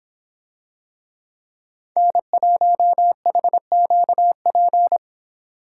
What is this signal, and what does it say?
Morse code, heard as a keyed tone, about 26 wpm, 700 Hz. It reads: N1HQP